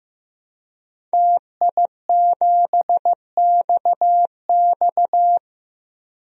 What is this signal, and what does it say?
Morse code, heard as a keyed tone, about 16 wpm, 700 Hz. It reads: TI7XX
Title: DXCC-100 Call Signs - 5 Characters - 15wpm